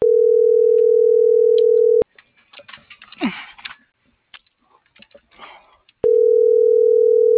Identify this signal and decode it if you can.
Ringback tone (North America), 440 + 480 Hz.